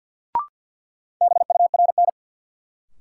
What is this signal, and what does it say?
Morse code, heard as a keyed tone, about 52 wpm, 700 Hz. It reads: BURN
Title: Sentences from Top 1300 Words - New Word Drill - 50wpm